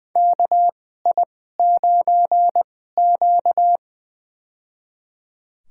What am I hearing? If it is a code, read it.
Morse code, heard as a keyed tone, about 20 wpm, spaced as for 14 wpm, 700 Hz. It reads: KI9Q
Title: USA Call Signs - 4 Characters - 2x Character Spacing - 20wpm